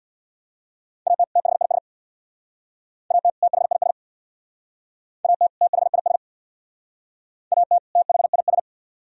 Morse code, heard as a keyed tone, about 65 wpm, 700 Hz. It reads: AT THIS AT THIS AT THIS AT THIS